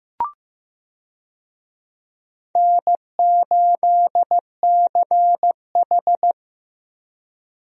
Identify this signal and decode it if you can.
Morse code, heard as a keyed tone, about 16 wpm, 700 Hz. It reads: N8CH
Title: USA Call Signs - 4 Characters - 15wpm